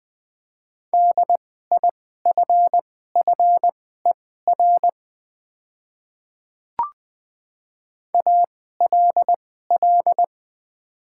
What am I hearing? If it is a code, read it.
Morse code, heard as a keyed tone, about 20 wpm, spaced as for 14 wpm, 700 Hz. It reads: DIFFER ALL
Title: 2x Character Spacing - Top 200 Words - 20wpm